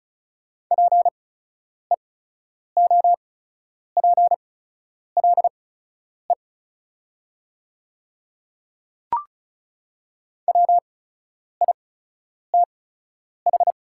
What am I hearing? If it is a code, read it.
Morse code, heard as a keyed tone, about 36 wpm, spaced as for 10 wpm, 690 Hz. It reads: PEOPLE WITH
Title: 8x Character Spacing - Top 100 Words - 35wpm